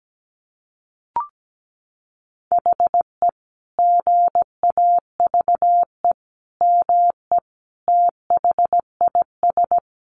Morse code, heard as a keyed tone, about 17 wpm, 700 Hz. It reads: HE GAVE ME THIS
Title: Sentences from Top 500 Words 17wpm